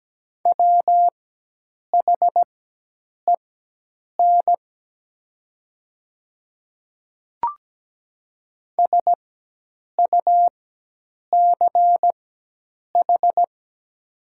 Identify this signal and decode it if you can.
Morse code, heard as a keyed tone, about 17 wpm, spaced as for 8 wpm, 700 Hz. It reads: WHEN SUCH
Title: 4x Character Spacing - Top 300 Words - 17wpm